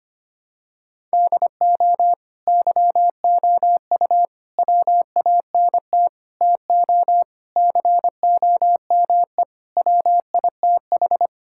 Morse code, heard as a keyed tone, about 25 wpm, 700 Hz. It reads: DO YOU WANT TO COME WITH
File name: practice / 25wpm / Sentences with 3-Word Phrases - 25wpm